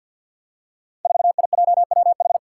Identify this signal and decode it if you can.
Morse code, heard as a keyed tone, about 53 wpm, 700 Hz. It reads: 4IJWH
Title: Five Challenging Letter-Number Combinations 50wpm